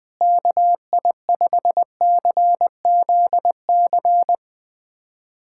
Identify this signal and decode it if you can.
Morse code, heard as a keyed tone, about 20 wpm, 700 Hz. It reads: KI5CZC
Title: USA Callsign DE Callsign - 6 Characters - 20wpm